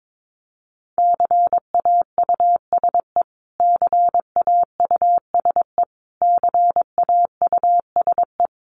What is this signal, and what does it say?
Morse code, heard as a keyed tone, about 22 wpm, 700 Hz. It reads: CAUSE CAUSE CAUSE
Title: Top 500 Words Repeated 22wpm